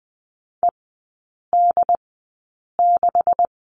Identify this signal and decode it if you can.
Morse code, heard as a keyed tone, about 20 wpm, spaced as for 8 wpm, 700 Hz. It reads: ED6
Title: 166 Lesson 27e - Number 7 - Three Characters 20wpm